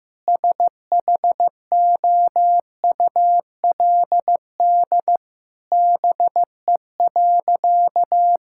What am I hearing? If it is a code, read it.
Morse code, heard as a keyed tone, about 15 wpm, 700 Hz. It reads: SHOULD BE.
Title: Sentences from Top 1000 Words - Review Part 6 - 15wpm